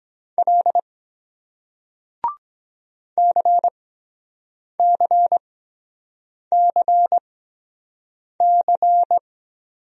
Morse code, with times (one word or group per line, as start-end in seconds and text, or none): 0.4-0.8 L
3.2-9.2 CCCC